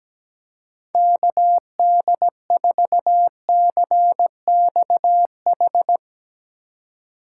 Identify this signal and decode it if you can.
Morse code, heard as a keyed tone, about 17 wpm, 700 Hz. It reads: KD4CXH
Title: USA Call Signs - 6 Characters - 4x Character Spacing - 17wpm